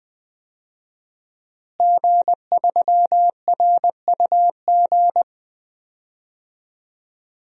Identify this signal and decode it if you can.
Morse code, heard as a keyed tone, about 20 wpm, 700 Hz. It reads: G3RUG
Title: DXCC-100 Call Signs - 5 Characters - Repeated 2x - 20wpm